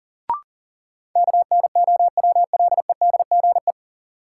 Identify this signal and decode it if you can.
Morse code, heard as a keyed tone, about 38 wpm, 700 Hz. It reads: KNOWLEDGE
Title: Sentences from Top 600 Words - New Word Drill - 40wpm